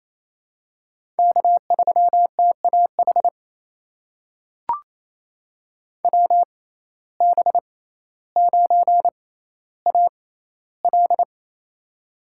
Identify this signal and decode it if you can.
Morse code, heard as a keyed tone, about 28 wpm, spaced as for 14 wpm, 700 Hz. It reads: K3TAH WB9AL
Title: USA Call Signs - 5 Characters - 6x Character Spacing - 28wpm